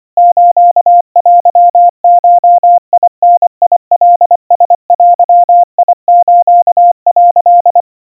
Morse code (Japanese Wabun code), heard as a keyed tone, about 24 wpm, 700 Hz. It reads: ステゴダカラデス」
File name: 90 cpm 700 Hz Crispy